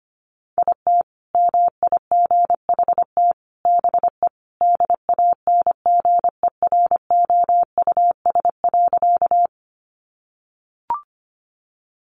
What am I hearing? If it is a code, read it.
Morse code, heard as a keyed tone, about 25 wpm, 700 Hz. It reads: IT MIGHT BE DANGEROUS.
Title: Sentences from Top 1200 Words 25wpm